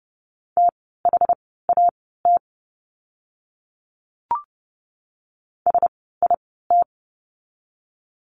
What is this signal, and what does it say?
Morse code, heard as a keyed tone, about 30 wpm, spaced as for 17 wpm, 700 Hz. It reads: THAT SIT
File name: course / 30:10wpm / 115 Lesson 19f - Letter w - Words 30wpm